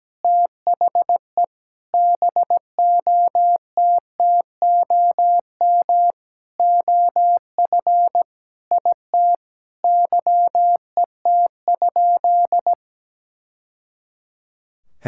Morse code, heard as a keyed tone, about 17 wpm, 700 Hz. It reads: THE BOTTOM OF IT YET?
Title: Sentences from Top 1000 Words - Review Part 6 - 17wpm